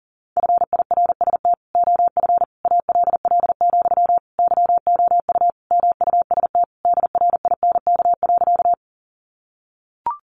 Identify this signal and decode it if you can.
Morse code, heard as a keyed tone, about 40 wpm, 700 Hz. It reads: FIRST OF ALL, YOU MUST DRINK.
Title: Sentences from Top 1500 Words 40wpm